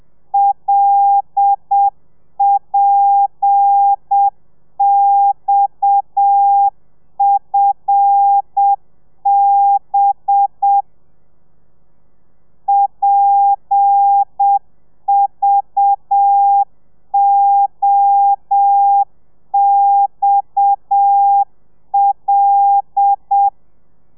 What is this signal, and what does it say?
Morse code, heard as a keyed tone, about 7 wpm, 800 Hz. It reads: LPXFB PVOXL
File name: telegrafia / i2tfj / Lez7 30